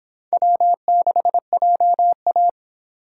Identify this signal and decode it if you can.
Morse code, heard as a keyed tone, about 26 wpm, 700 Hz. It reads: W6JA